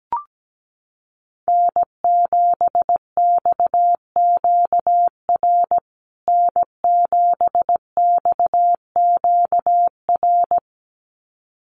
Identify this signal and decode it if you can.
Morse code, heard as a keyed tone, about 17 wpm, 700 Hz. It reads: N7XQR N7XQR
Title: USA Call Signs Repeated 2x 17wpm